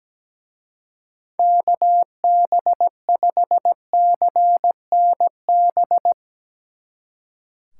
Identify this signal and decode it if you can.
Morse code, heard as a keyed tone, about 18 wpm, 700 Hz. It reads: KB5CNB
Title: Speed Racing - USA 2x3 Call Signs 17wpm